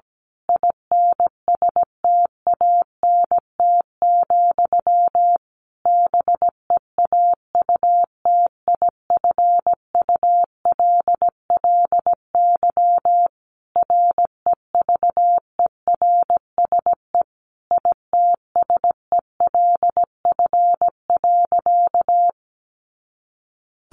Morse code, heard as a keyed tone, about 17 wpm, 700 Hz. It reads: INSTANT, BEAUTIFULLY REVERSE ITSELF.
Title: A Sound Of Thunder 17wpm